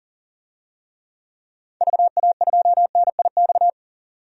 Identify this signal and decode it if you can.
Morse code, heard as a keyed tone, about 40 wpm, 700 Hz. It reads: VA2NIX
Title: DXCC-100 Call Signs - 6 Characters - 40wpm